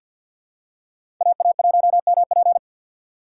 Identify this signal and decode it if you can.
Morse code, heard as a keyed tone, about 47 wpm, 700 Hz. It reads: AA1MP
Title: USA Call Signs - 5 Characters - 2x Character Spacing - 50wpm